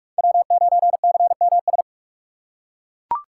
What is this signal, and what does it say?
Morse code, heard as a keyed tone, about 41 wpm, 700 Hz. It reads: W9CMS
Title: USA Call Signs - 5 Characters - 4x Character Spacing - 45wpm